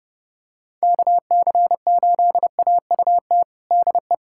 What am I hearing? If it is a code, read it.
Morse code, heard as a keyed tone, about 32 wpm, 700 Hz. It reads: KC8AUT DE
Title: USA Callsign DE Callsign - 6 Characters - 30wpm